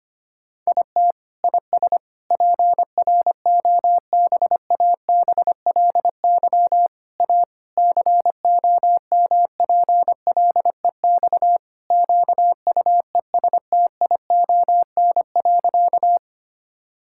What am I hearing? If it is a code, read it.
Morse code, heard as a keyed tone, about 25 wpm, 700 Hz. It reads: IT IS PROBABLY A COMPLEX QUESTION.
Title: Sentences from Top 1000 Words - Review Part 6 - 25wpm